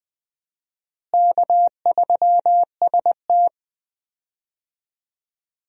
Morse code, heard as a keyed tone, about 20 wpm, 700 Hz. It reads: K3ST
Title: USA Call Signs - 4 Characters - Repeated 2x - 20wpm